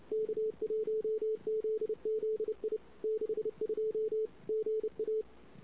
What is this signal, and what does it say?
Morse code, heard as a keyed tone, about 30 wpm, 440 Hz. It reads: K1ZZI 62 GA
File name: CW / 2008-ss-cw-k5tr-op-n5rz -16:00